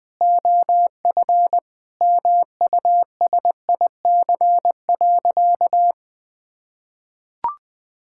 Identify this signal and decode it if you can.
Morse code, heard as a keyed tone, about 21 wpm, 700 Hz. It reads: OF MUSIC.
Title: Sentences with 3-Word Phrases - 20wpm